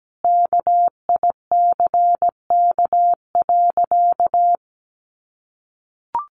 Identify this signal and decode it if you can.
Morse code, heard as a keyed tone, about 17 wpm, 700 Hz. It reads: KICK.
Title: Sentences from Top 1300 Words 17wpm